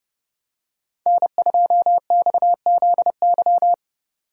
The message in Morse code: N2XZY

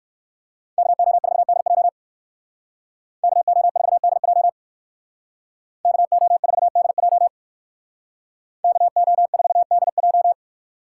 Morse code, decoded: KO4DJ KO4DJ KO4DJ KO4DJ

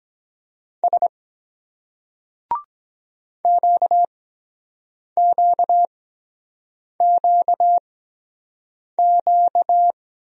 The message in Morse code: S QQQQ

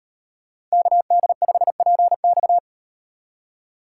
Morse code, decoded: KD5PX

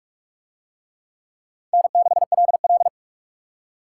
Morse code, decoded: N6LL

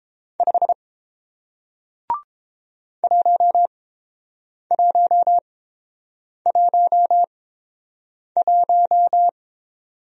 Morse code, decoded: 5 1111

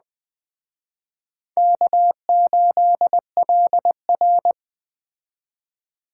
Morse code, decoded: K8LR

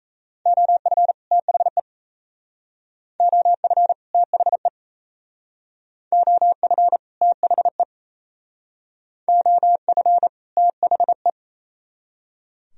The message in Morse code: OF THE OF THE OF THE OF THE

